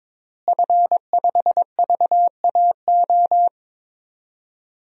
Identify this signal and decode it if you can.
Morse code, heard as a keyed tone, about 22 wpm, 700 Hz. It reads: F5VAO
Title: DXCC-100 Call Signs - 5 Characters - Repeated 3x - 22wpm